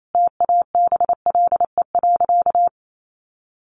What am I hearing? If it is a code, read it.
Morse code, heard as a keyed tone, about 28 wpm, 700 Hz. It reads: TABLE.
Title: Sentences from Top 1500 Words 28wpm